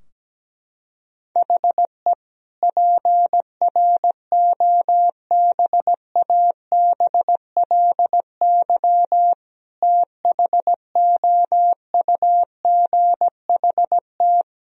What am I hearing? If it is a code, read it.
Morse code, heard as a keyed tone, about 17 wpm, 700 Hz. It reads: HE PROBABLY THOUGHT
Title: Sentences from Top 1000 Words - Review Part 4 - 17wpm